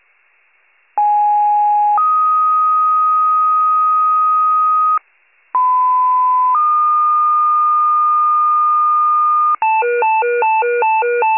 Two-tone page, heard as a one-second tone, then a three-second tone, about 830 then 1250 Hz, about 1010 then 1220 Hz. It is heard from 1.0 to 9.6 s.
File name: wcalert